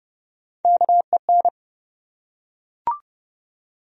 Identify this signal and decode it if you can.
Morse code, heard as a keyed tone, about 31 wpm, 700 Hz. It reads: KEN
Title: USA Common Names - Max 3 Letters - 30wpm